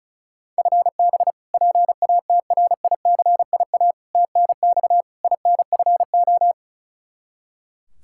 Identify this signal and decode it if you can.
Morse code, heard as a keyed tone, about 36 wpm, 700 Hz. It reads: FB PATRICIA TNX INFO
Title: Practice QSO - Exchange 3 - More Info - 35wpm